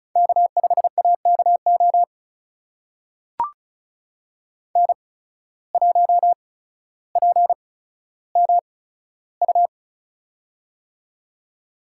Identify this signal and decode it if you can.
Morse code, heard as a keyed tone, about 36 wpm, spaced as for 15 wpm, 700 Hz. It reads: K5AKO N1PMU
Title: USA Call Signs - 5 Characters - 8x Character Spacing - 35wpm